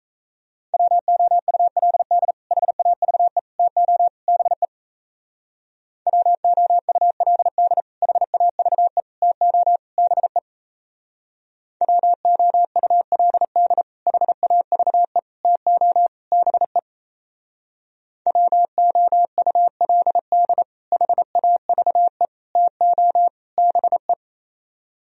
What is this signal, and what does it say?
Morse code, heard as a keyed tone, about 33 wpm, 700 Hz. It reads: WOULD HAVE TO BE WOULD HAVE TO BE WOULD HAVE TO BE WOULD HAVE TO BE
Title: Speed-Racing - Sets of 4 Words 28wpm